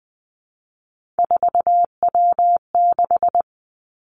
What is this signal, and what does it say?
Morse code, heard as a keyed tone, about 20 wpm, 700 Hz. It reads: 4W6
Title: Three Challenging Letter-Number Combinations 20wpm